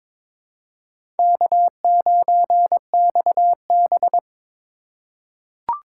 Morse code, heard as a keyed tone, about 22 wpm, 700 Hz. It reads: K9XB
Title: USA Call Signs - 2x Character Spacing - 22wpm